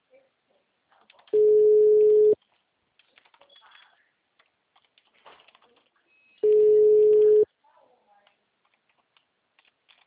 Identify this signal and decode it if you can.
Ringback tone (Europe, 425 Hz), with speech.